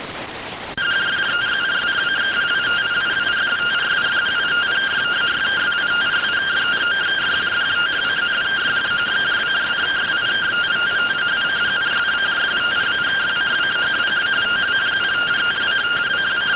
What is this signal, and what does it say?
Two-tone teleprinter signal, about 170 Hz shift.